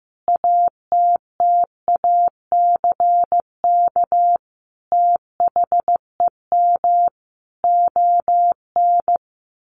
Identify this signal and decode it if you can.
Morse code, heard as a keyed tone, about 15 wpm, 700 Hz. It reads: ATTACK THEM ON
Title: Sentences from Top 1500 Words 15wpm